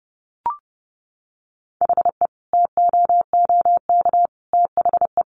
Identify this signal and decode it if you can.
Morse code, heard as a keyed tone, about 28 wpm, 700 Hz. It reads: HE TOOK THE
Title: Sentences from Top 1300 Words 30wpm